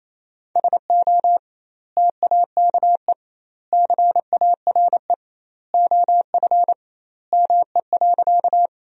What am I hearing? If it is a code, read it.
Morse code, heard as a keyed tone, about 27 wpm, 700 Hz. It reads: SO TAKE CARE OF ME.
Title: 2x Word-Spacing - Sentences from Top 300 Words - 28wpm